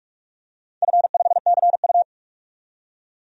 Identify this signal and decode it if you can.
Morse code, heard as a keyed tone, about 42 wpm, 700 Hz. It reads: F5CU
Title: DXCC-100 Call Signs - 4 Characters - 45wpm